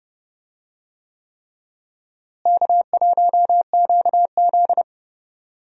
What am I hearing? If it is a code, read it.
Morse code, heard as a keyed tone, about 30 wpm, 700 Hz. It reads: K1QZ
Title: USA Call Signs - 4 Characters - Repeated 3x - 30wpm